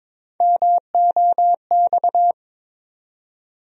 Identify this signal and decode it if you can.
Morse code, heard as a keyed tone, about 21 wpm, 700 Hz. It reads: MOX
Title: USA Call Sign Postfixes - 3 Letters 22wpm